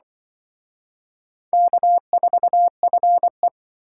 Morse code, heard as a keyed tone, about 24 wpm, 700 Hz. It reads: K4FE
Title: Speed Racing - USA Call Signs 20wpm